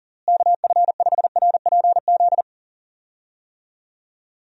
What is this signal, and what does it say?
Morse code, heard as a keyed tone, about 40 wpm, 700 Hz. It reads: KF5RPZ
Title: USA Call Signs - 6 Characters - Repeated 2x - 40wpm